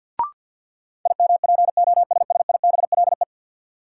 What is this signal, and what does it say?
Morse code, heard as a keyed tone, about 50 wpm, 700 Hz. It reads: IMPOSSIBLE
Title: Sentences from Top 1100 Words - New Word Drill Alt - 50wpm